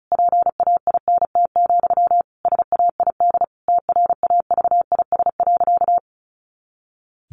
Morse code, heard as a keyed tone, about 34 wpm, 700 Hz. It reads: PAINT, SAID TRAVIS.
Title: A Sound Of Thunder 35wpm